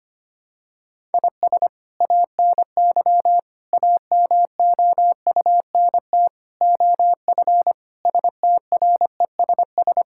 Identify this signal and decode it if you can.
Morse code, heard as a keyed tone, about 25 wpm, 700 Hz. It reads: IS ANY AMOUNT OF STRESS
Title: Sentences from Top 1000 Words - Review Part 6 - 25wpm